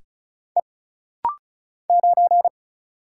Morse code, heard as a keyed tone, about 36 wpm, 700 Hz. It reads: E 9
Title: Single Letter-Number Alt - Rapid-Fire 35wpm